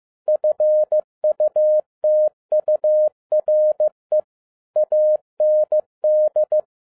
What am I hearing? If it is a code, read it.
Morse code, heard as a keyed tone, about 15 wpm, 610 Hz. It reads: FUTURE AND
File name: FIELD DAY CONTEST QRV FROM SAKAE VILLAGE 15wpm